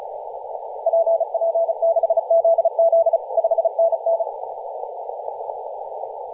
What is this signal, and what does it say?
Morse code, heard as a keyed tone, about 35 wpm, 660 Hz, with radio noise. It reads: PP6ZZ 5NN
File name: posts / 2016-nov-11-2050z-tl8ao-10-112mhz-cw